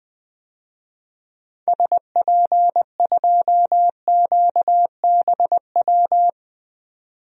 Morse code, heard as a keyed tone, about 20 wpm, 700 Hz. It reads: SP2QBW
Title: DXCC-100 Call Signs - 6 Characters - 20wpm